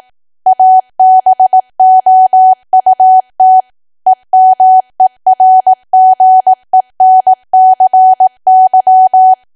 Morse code, heard as a keyed tone, about 18 wpm, 750 Hz. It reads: ABOUT EMERGENCY
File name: Archive / 18 WPM / 18WPM